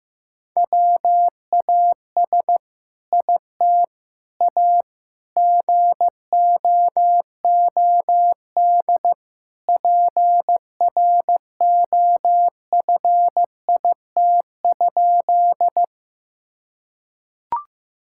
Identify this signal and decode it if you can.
Morse code, heard as a keyed tone, about 15 wpm, 700 Hz. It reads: WAS IT A GOOD PROFIT?
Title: Sentences from Top 1000 Words - Review Part 5 - 15wpm